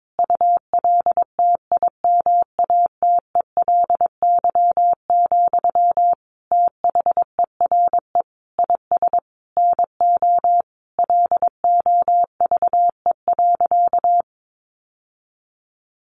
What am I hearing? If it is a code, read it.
Morse code, heard as a keyed tone, about 22 wpm, 700 Hz. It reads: ULTIMATELY, THERE IS NO LOVE.
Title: Sentences from Top 1200 Words 22wpm